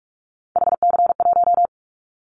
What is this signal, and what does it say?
Morse code, heard as a keyed tone, about 46 wpm, 700 Hz. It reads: HC1
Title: Callsign Prefixes for Top 100 DXCC - 45wpm